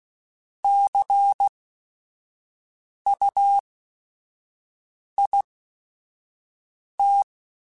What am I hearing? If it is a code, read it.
Morse code, heard as a keyed tone, about 16 wpm, spaced as for 5 wpm, 800 Hz. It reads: CUIT
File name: Technical Subjects / Technical Subjects-mp3s / XC @5